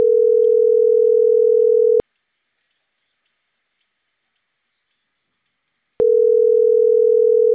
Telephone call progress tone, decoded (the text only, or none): ringback tone